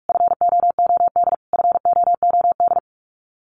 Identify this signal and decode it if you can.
Morse code, heard as a keyed tone, about 44 wpm, 700 Hz. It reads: FOOD FOOD
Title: Top 500 Words Repeated 45wpm